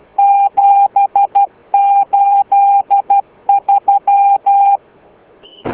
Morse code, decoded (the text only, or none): 783